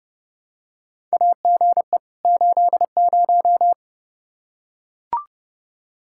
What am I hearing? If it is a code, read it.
Morse code, heard as a keyed tone, about 30 wpm, 700 Hz. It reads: AGE 80
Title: K1USN SST QSO Practice 30wpm